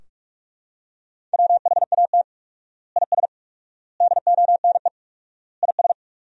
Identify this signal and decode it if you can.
Morse code, heard as a keyed tone, about 41 wpm, 700 Hz. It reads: WHAT IS DONE IS
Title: 4x Word-Spacing - Sentences from Top 500 Words - 45wpm